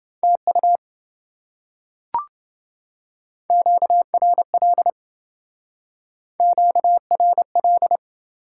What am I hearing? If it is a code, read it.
Morse code, heard as a keyed tone, about 27 wpm, 700 Hz. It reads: TU QRL QRL